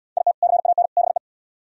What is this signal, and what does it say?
Morse code, heard as a keyed tone, about 71 wpm, 700 Hz. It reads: IT CAN BE